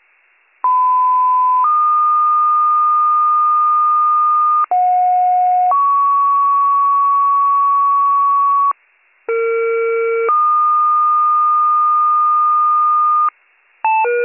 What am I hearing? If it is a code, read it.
Two-tone page, heard as a one-second tone, then a three-second tone, about 1010 then 1220 Hz, about 730 then 1120 Hz, about 460 then 1220 Hz. It is heard from 0.6 to 13.3 s.